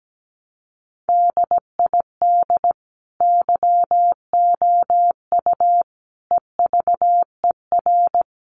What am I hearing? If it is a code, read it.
Morse code, heard as a keyed tone, about 17 wpm, 700 Hz. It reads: DID YOU EVER
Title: Sentences from Top 700 Words 17wpm